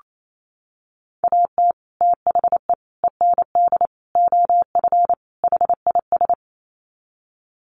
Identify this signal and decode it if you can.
Morse code, heard as a keyed tone, about 28 wpm, 700 Hz. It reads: AT THE END OF HIS
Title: Sets of 5 Words Encore 28wpm